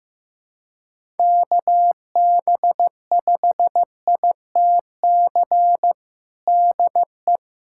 Morse code, heard as a keyed tone, about 15 wpm, 700 Hz. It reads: KB5ITC DE
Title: USA Callsign DE Callsign - 6 Characters - 15wpm